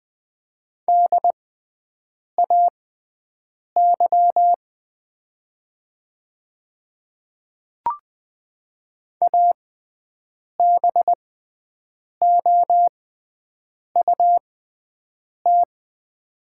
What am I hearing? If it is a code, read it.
Morse code, heard as a keyed tone, about 20 wpm, spaced as for 7 wpm, 700 Hz. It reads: DAY ABOUT
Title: 6x Character Spacing - Top 100 Words - 20wpm